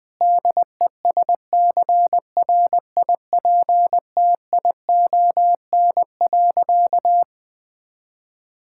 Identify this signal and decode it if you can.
Morse code, heard as a keyed tone, about 21 wpm, 700 Hz. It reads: DESCRIPTION.
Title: Sentences from Top 1600 Words 20wpm